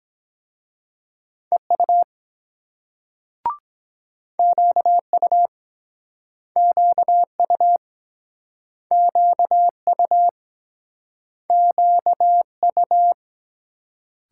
Morse code, heard as a keyed tone, about 24 wpm, 700 Hz. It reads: EU QU QU QU QU